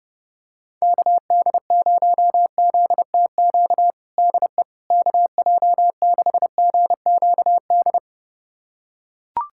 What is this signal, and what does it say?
Morse code, heard as a keyed tone, about 30 wpm, 700 Hz. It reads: KD0ZTQ DE KJ6GQD
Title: USA Callsign DE Callsign 30wpm